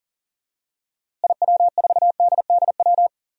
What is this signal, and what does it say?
Morse code, heard as a keyed tone, about 40 wpm, 700 Hz. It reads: IW4DDW
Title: DXCC-100 Call Signs - 6 Characters - Repeated 2x - 40wpm